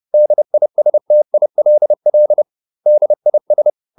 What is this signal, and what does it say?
Morse code, heard as a keyed tone, about 30 wpm, 600 Hz. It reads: DISTILL DIS